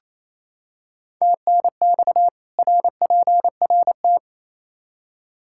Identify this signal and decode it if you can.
Morse code, heard as a keyed tone, about 28 wpm, 700 Hz. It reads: TNX RPRT
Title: K1USN SST QSO Practice 28wpm